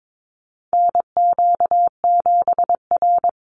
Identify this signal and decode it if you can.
Morse code, heard as a keyed tone, about 22 wpm, 700 Hz. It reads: NQ7R
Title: NA SOTA Chaser Callsigns 22wpm